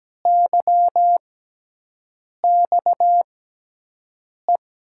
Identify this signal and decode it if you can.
Morse code, heard as a keyed tone, about 17 wpm, spaced as for 6 wpm, 700 Hz. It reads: YXE
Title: USA Call Signs - 6 Characters - 6x Character Spacing - 17wpm